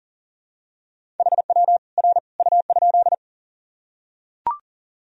Morse code, heard as a keyed tone, about 40 wpm, 700 Hz. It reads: HW R U?
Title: Sentences from Top 100 Words and Abbreviations 40wpm